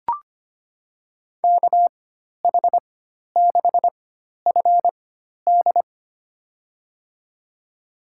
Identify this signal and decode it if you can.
Morse code, heard as a keyed tone, about 25 wpm, spaced as for 12 wpm, 700 Hz. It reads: KH6FD
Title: USA Call Signs - 4x Character Spacing - 25wpm